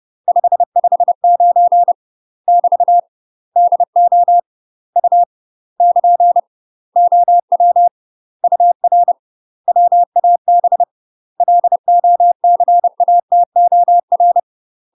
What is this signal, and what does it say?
Morse code, heard as a keyed tone, about 30 wpm, 700 Hz. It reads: 559 <BT> DO U <KN> OW UR WAB LOCATOR
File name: UK MASTER QSOs TRAINING / 30WPM / UK MASTER QSOs 0051